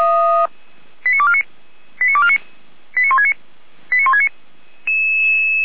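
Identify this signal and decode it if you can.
Five-tone selective call (ZVEI), heard as quick runs of five tones, tones of about 70 ms. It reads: 89249 89249 89169 89169
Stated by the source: ZVEI five-tone selective call received via SDR.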